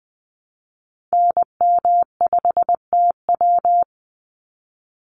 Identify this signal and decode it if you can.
Morse code, heard as a keyed tone, about 20 wpm, 700 Hz. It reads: NM5TW
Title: NA SOTA Activator Callsigns 20wpm